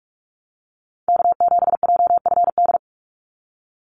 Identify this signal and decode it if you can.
Morse code, heard as a keyed tone, about 44 wpm, 700 Hz. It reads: K7JFD
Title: NA SOTA Chaser Callsigns 45wpm